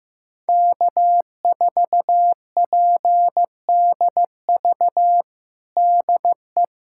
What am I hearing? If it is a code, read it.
Morse code, heard as a keyed tone, about 15 wpm, 700 Hz. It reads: K4PDV DE
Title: USA Callsign DE Callsign 15wpm